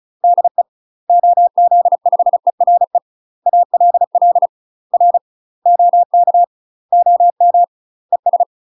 Morse code, heard as a keyed tone, about 36 wpm, 700 Hz. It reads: DE OZ5ERE ALL R OK OM ES
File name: UK MASTER QSOs TRAINING / 35WPM / UK MASTER QSOs 0028